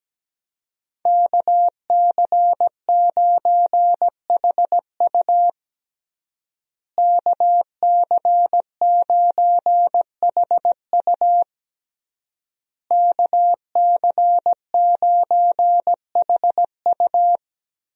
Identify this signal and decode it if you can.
Morse code, heard as a keyed tone, about 17 wpm, 700 Hz. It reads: KC9HU KC9HU KC9HU
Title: USA Call Signs - 5 Characters - Repeated 3x - 17wpm